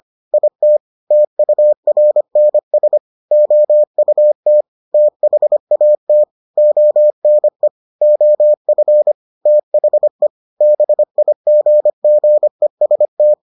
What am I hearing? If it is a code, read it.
Morse code, heard as a keyed tone, about 25 wpm, 600 Hz. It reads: IT TURNS OUT THAT ONE OF THE BIGGEST